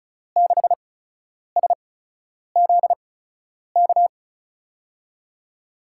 Morse code, heard as a keyed tone, about 36 wpm, spaced as for 10 wpm, 700 Hz. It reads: 6SZK
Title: USA Call Signs - 6 Characters - 8x Character Spacing - 35wpm